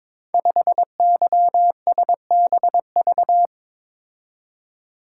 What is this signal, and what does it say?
Morse code, heard as a keyed tone, about 23 wpm, 700 Hz. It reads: 5YSBV
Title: Code Groups - 5 Letter:Numbers - 22wpm